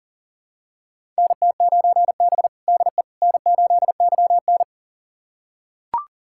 Morse code, heard as a keyed tone, about 40 wpm, 700 Hz. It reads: NT9B DE N8YN
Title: USA Callsign DE Callsign - 4 Characters - 40wpm